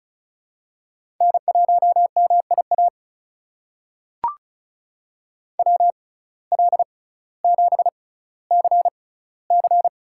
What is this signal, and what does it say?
Morse code, heard as a keyed tone, about 36 wpm, spaced as for 18 wpm, 700 Hz. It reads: N1MIA WL7CC
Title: USA Call Signs - 5 Characters - 6x Character Spacing - 35wpm